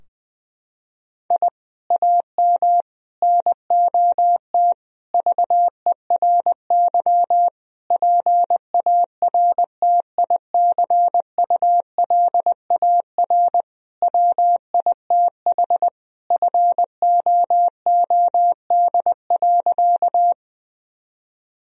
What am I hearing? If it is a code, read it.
Morse code, heard as a keyed tone, about 20 wpm, 700 Hz. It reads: I AM NOT VERY PARTICULAR WITH FOOD.
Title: Sentences from Top 1000 Words - Review Part 4 - 20wpm